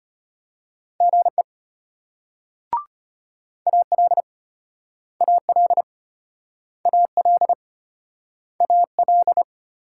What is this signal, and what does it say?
Morse code, heard as a keyed tone, about 34 wpm, 700 Hz. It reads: GE AL AL AL AL